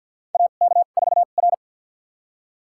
Morse code, heard as a keyed tone, about 52 wpm, spaced as for 37 wpm, 700 Hz. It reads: AK4R
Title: USA Call Signs - 4 Characters - 2x Character Spacing - 50wpm